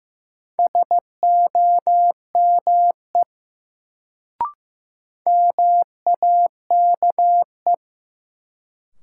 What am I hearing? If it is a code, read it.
Morse code, heard as a keyed tone, about 15 wpm, 700 Hz. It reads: SOME MAKE
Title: Sentences from Top 100 Words - New Word Drill - 15wpm